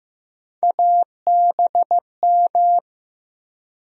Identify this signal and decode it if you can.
Morse code, heard as a keyed tone, about 16 wpm, 700 Hz. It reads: ABM
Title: USA Call Sign Postfixes - 3 Letters 15wpm